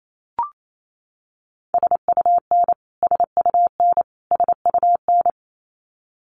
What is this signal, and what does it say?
Morse code, heard as a keyed tone, about 28 wpm, 700 Hz. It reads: SUN SUN SUN
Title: Top 500 Words Repeated 28wpm